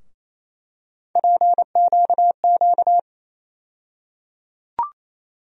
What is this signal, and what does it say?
Morse code, heard as a keyed tone, about 28 wpm, 700 Hz. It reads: PQQ
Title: USA Call Sign Postfixes - 3 Letters 28wpm